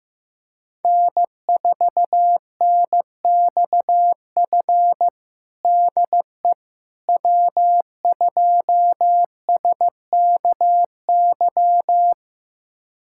Morse code, heard as a keyed tone, about 15 wpm, 700 Hz. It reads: N4NXF DE W2SKY
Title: USA Callsign DE Callsign - 5 Characters - 15wpm